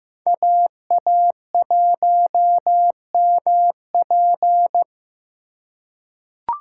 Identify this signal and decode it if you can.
Morse code, heard as a keyed tone, about 15 wpm, 700 Hz. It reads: AA1MP
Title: USA Call Signs - 5 Characters - 2x Character Spacing - 15wpm